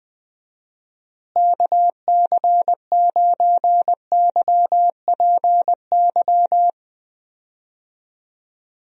Morse code, heard as a keyed tone, about 20 wpm, 700 Hz. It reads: KC9YPY
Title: USA Call Signs - 6 Characters - Repeated 2x - 20wpm